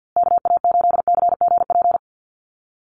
Morse code, heard as a keyed tone, about 50 wpm, 700 Hz. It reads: KA8CGP
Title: USA Call Signs Repeated 2x 50wpm